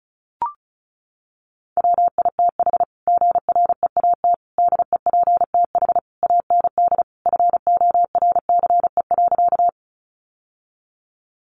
Morse code, heard as a keyed tone, about 34 wpm, 700 Hz. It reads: WITH GREAT DEPTH AND FORCE.
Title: Sentences from Top 1200 Words 35wpm